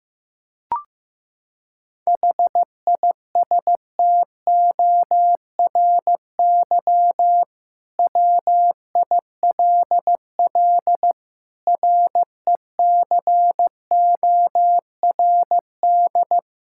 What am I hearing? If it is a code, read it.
Morse code, heard as a keyed tone, about 15 wpm, 700 Hz. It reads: HISTORY WILL RECORD
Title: Sentences from Top 1600 Words 15wpm